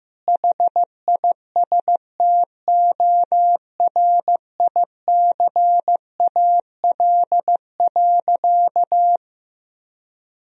Morse code, heard as a keyed tone, about 15 wpm, 700 Hz. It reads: HISTORICAL.
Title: Sentences from Top 1000 Words - Review Part 4 - 15wpm